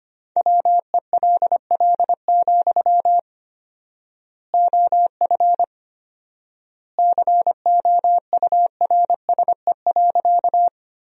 Morse code, heard as a keyed tone, about 25 wpm, 700 Hz. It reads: WELL, OF COURSE.
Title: 4x Word-Spacing - Sentences from Top 500 Words - 25wpm